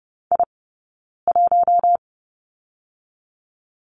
Morse code, heard as a keyed tone, about 30 wpm, 700 Hz. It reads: I 1
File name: Lesson 15d - Number 5 - Two Characters 30wpm